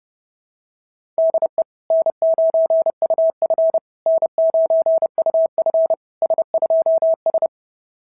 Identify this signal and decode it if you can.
Morse code, heard as a keyed tone, about 32 wpm, 660 Hz. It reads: DE N9UF N9UF S2S
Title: SOTA QSO Practice 30wpm